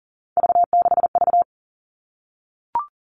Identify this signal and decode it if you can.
Morse code, heard as a keyed tone, about 40 wpm, 700 Hz. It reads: V6V
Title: All Cut Numbers - 3-Digit 40wpm